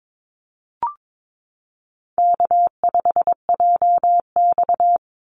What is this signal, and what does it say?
Morse code, heard as a keyed tone, about 22 wpm, 700 Hz. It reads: K5JX
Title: Call Signs Repeated 2x - 22wpm